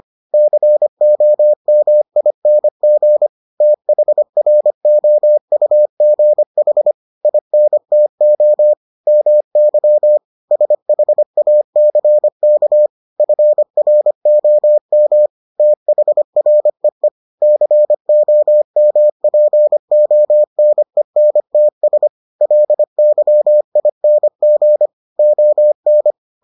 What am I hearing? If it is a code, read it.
Morse code, heard as a keyed tone, about 25 wpm, 600 Hz. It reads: COMING THROUGH INTO MY SHACK FROM THREE COMPONENTS LYING ON